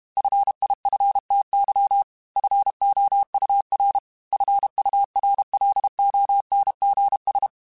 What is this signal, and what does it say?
Morse code, heard as a keyed tone, about 32 wpm, 800 Hz. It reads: FIFTY FOUR FURLONGS
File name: F Runs / F Runs mp3s / F Runs @30